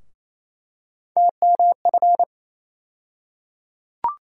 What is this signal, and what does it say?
Morse code, heard as a keyed tone, about 28 wpm, 700 Hz. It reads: TMF
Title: USA Call Sign Postfixes - 3 Letters 28wpm